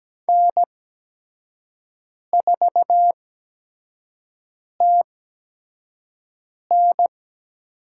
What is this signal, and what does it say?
Morse code, heard as a keyed tone, about 17 wpm, spaced as for 5 wpm, 700 Hz. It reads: N4TN
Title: USA Call Signs - 5 Characters - 8x Character Spacing - 17wpm